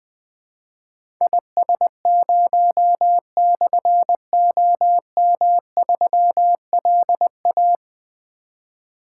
Morse code, heard as a keyed tone, about 20 wpm, 700 Hz. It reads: IS0/OM3LA
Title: Difficult Call Signs 20wpm